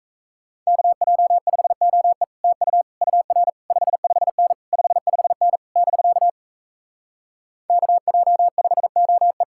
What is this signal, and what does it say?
Morse code, heard as a keyed tone, about 38 wpm, 700 Hz. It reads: KJ5OE TU UR 55N 55N <BK> KJ5OE